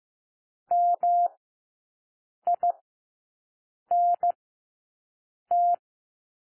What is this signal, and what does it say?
Morse code, heard as a keyed tone, about 15 wpm, spaced as for 6 wpm, 700 Hz. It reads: MINT
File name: cwmp3 / W1 SOUND WORDS 15 wpm